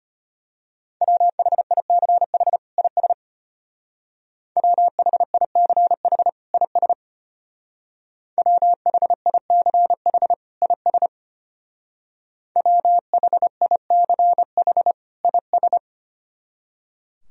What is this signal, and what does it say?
Morse code, heard as a keyed tone, about 31 wpm, 700 Hz. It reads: WHICH IS WHICH IS WHICH IS WHICH IS
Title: Speed-Racing - Sets of 2 Words 25wpm